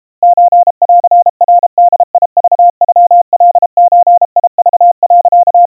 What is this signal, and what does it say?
Morse code (Japanese Wabun code), heard as a keyed tone, about 32 wpm, 700 Hz. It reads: ソンナボクノカゾク、
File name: kawa / Morse sample / kamyu 120 cpm 700 Hz Crispy